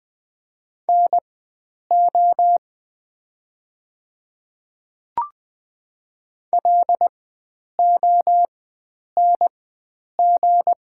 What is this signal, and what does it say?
Morse code, heard as a keyed tone, about 20 wpm, spaced as for 9 wpm, 700 Hz. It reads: NO LONG
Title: 4x Character Spacing - Top 100 Words - 20wpm